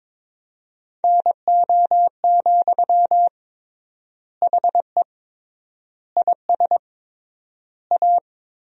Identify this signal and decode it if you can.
Morse code, heard as a keyed tone, about 22 wpm, 700 Hz. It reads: NO, HE IS A
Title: 3x Word-Spacing - Sentences from Top 400 Words - 22wpm